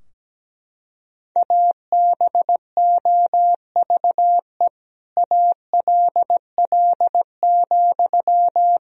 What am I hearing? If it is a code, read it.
Morse code, heard as a keyed tone, about 17 wpm, 700 Hz. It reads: ABOVE ALL,
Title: Sentences from Top 1000 Words - Review Part 2 - 17wpm